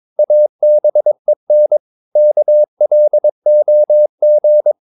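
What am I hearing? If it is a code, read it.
Morse code, heard as a keyed tone, about 22 wpm, 600 Hz. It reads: ABEN KLOG
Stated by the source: Morse practice of short Danish words at 22 wpm, from CW med Gnister.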